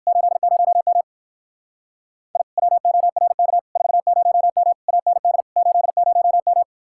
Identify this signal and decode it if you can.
Morse code, heard as a keyed tone, about 58 wpm, 700 Hz. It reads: 80M I WORK 40M AND 80M